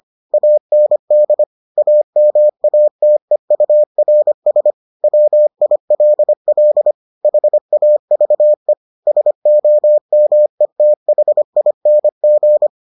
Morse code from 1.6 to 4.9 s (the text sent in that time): AMATEURS